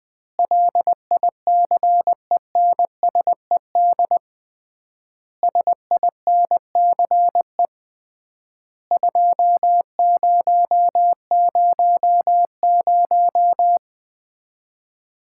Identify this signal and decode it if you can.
Morse code, heard as a keyed tone, about 20 wpm, 700 Hz. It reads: LICENSED SINCE 2000
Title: Common QSO Phrases - 3x Word-Spacing - 20wpm